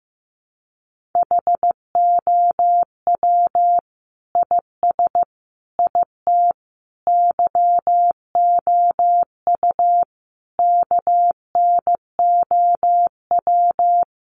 Morse code, heard as a keyed tone, about 15 wpm, 700 Hz. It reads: HOW IS IT YOU KNOW
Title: Sentences from Top 700 Words 15wpm